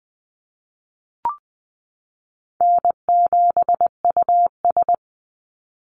Morse code, heard as a keyed tone, about 20 wpm, 700 Hz. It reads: N7US
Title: Call Signs Repeated 2x - 20wpm